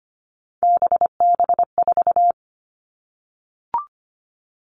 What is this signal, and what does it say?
Morse code, heard as a keyed tone, about 25 wpm, 700 Hz. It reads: BB4
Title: All Cut Numbers - 3-Digit 25wpm